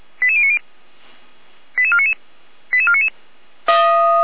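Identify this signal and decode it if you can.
Five-tone selective call (ZVEI), heard as quick runs of five tones, tones of about 70 ms. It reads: 88090 88499 88499
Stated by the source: ZVEI five-tone selective call received by SDR, address 88012, from a server of Feuerwehr Orlamünde (fire brigade).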